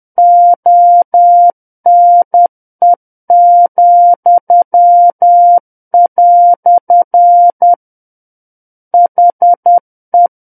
Morse code, heard as a keyed tone, about 10 wpm, 700 Hz. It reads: ONE," HE
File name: Book One / 01-Chapter 1 - 10 WPM